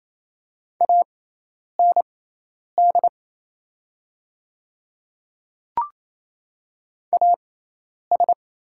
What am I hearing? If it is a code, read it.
Morse code, heard as a keyed tone, about 27 wpm, spaced as for 10 wpm, 700 Hz. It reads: AND AS
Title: 6x Character Spacing - Top 100 Words - 28wpm